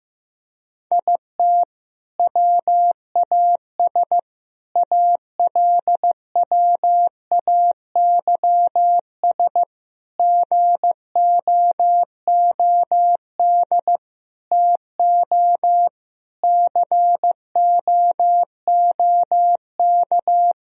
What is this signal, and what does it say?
Morse code, heard as a keyed tone, about 15 wpm, 700 Hz. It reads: IT WAS ALWAYS GOOD TO COOK
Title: Sentences from Top 1000 Words - Review Part 4 - 15wpm